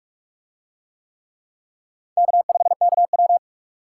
Morse code, heard as a keyed tone, about 39 wpm, 700 Hz. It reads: K5KW